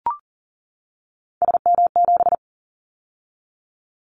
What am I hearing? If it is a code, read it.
Morse code, heard as a keyed tone, about 40 wpm, 700 Hz. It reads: SM7
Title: Callsign Prefixes for Top 25 DXCC - 40wpm